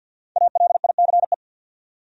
Morse code, heard as a keyed tone, about 52 wpm, 700 Hz. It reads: ALICE
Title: USA Common Names - Max 5 Letters - 50wpm